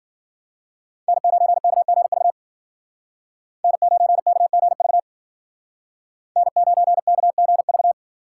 Morse code, heard as a keyed tone, about 52 wpm, 700 Hz. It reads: N9KGV N9KGV N9KGV